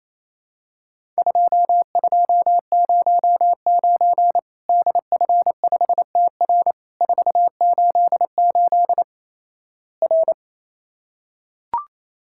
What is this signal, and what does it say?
Morse code, heard as a keyed tone, about 27 wpm, 700 Hz. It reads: 2209 DF5TR 488 R
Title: Worked All Europe DX Contest - QTC Exchange - 28wpm